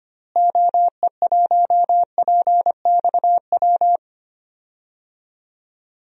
Morse code, heard as a keyed tone, about 25 wpm, 700 Hz. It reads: OE1PXW
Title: DXCC-100 Call Signs - 6 Characters - Repeated 2x - 25wpm